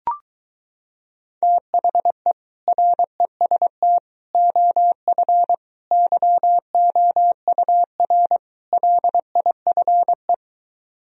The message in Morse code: THE REST OF YOUR LIFE

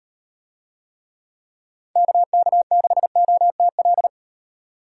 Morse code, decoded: KK6OTL

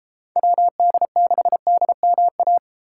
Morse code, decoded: WD6DMA

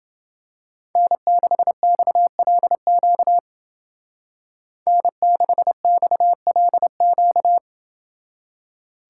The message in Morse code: N6XLQ N6XLQ

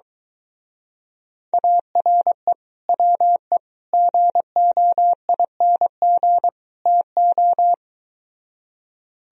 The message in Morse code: ARE WE GOING TO

